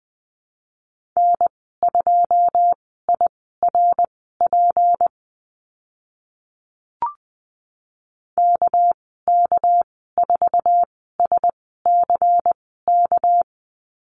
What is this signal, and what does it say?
Morse code, heard as a keyed tone, about 20 wpm, spaced as for 14 wpm, 700 Hz. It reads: N2IRP KK4SCK
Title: 200 Lesson 32g - Letter k - Call Signs 20wpm